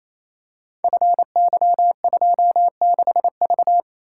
Morse code, heard as a keyed tone, about 27 wpm, 700 Hz. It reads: FY26V
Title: Code Groups - 5 Letter:Numbers - 28wpm